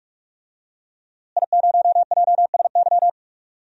Morse code, heard as a keyed tone, about 46 wpm, 700 Hz. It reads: I0JSY
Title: DXCC-100 Call Signs - 5 Characters - Repeated 3x - 45wpm